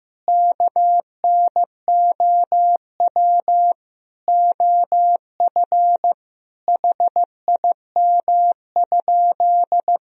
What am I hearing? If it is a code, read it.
Morse code, heard as a keyed tone, about 15 wpm, 700 Hz. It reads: KNOW OF HIM?
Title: Sentences from Top 100 Words and Abbreviations 15wpm